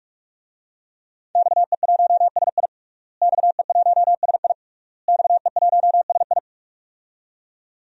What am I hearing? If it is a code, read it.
Morse code, heard as a keyed tone, about 46 wpm, 700 Hz. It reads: XE1SI XE1SI XE1SI
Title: DXCC-100 Call Signs - 5 Characters - Repeated 3x - 45wpm